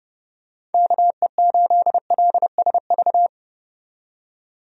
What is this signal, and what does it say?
Morse code, heard as a keyed tone, about 30 wpm, 700 Hz. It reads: KE8LSV
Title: USA Call Signs - 6 Characters - 30wpm